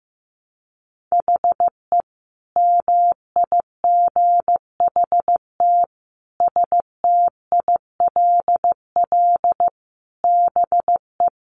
Morse code, heard as a keyed tone, about 15 wpm, 700 Hz. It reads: HE MIGHT STILL BE
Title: Sentences from Top 300 Words 15wpm